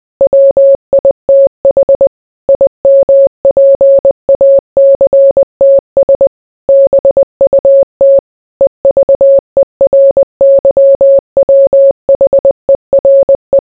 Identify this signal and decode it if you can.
Morse code, heard as a keyed tone, about 20 wpm, 550 Hz. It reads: WITH IMPACTS BUT EVERYWHERE